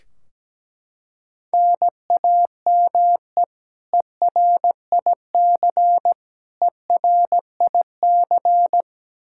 Morse code, heard as a keyed tone, about 17 wpm, 700 Hz. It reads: NAME ERIC ERIC
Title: Practice QSO - Exchange 1 - Beginning - 17wpm